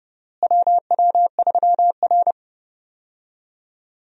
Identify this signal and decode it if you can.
Morse code, heard as a keyed tone, about 29 wpm, 700 Hz. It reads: WW3R
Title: USA Call Signs - 4 Characters - Repeated 2x - 30wpm